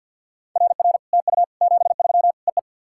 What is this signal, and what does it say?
Morse code, heard as a keyed tone, about 50 wpm, 700 Hz. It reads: RR TU 73 EE
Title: SOTA QSO Practice 50wpm